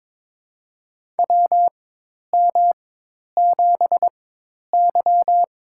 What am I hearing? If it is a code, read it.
Morse code, heard as a keyed tone, about 22 wpm, spaced as for 10 wpm, 700 Hz. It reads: WM7Y
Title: USA Call Signs - 4 Characters - 4x Character Spacing - 22wpm